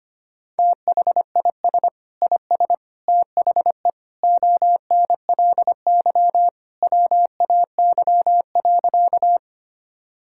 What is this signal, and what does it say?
Morse code, heard as a keyed tone, about 25 wpm, 700 Hz. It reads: THIS IS THE ONLY WAY.
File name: practice / 25wpm / Sentences with 2-Word Phrases - 25wpm